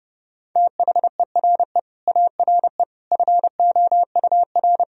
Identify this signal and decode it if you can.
Morse code, heard as a keyed tone, about 30 wpm, 700 Hz. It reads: THERE ARE FOUR
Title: Sentences from Top 1000 Words - Review Part 2 - 30wpm